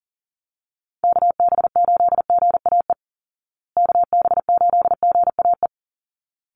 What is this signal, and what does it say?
Morse code, heard as a keyed tone, about 40 wpm, 700 Hz. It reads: KB8GAE KB8GAE
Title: Call Signs Repeated 3x - 40wpm